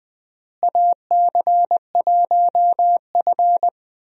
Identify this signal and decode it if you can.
Morse code, heard as a keyed tone, about 20 wpm, 700 Hz. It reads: AC1F
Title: USA Call Signs - 4 Characters - Repeated 2x - 20wpm